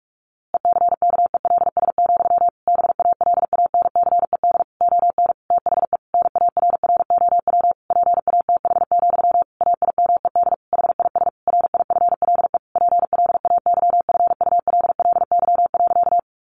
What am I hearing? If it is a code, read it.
Morse code, heard as a keyed tone, about 44 wpm, 700 Hz. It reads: ECKELS, BALANCED ON THE NARROW PATH, AIMED HIS RIFLE PLAYFULLY.
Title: A Sound Of Thunder 45wpm